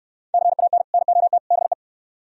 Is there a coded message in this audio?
Yes: Morse code from 0.3 to 1.7 s, heard as a keyed tone, about 64 wpm, 700 Hz. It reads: CAN NOT BE